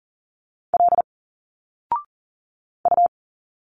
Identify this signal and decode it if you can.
Morse code, heard as a keyed tone, about 40 wpm, 700 Hz. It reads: L U
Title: Single Letters 40wpm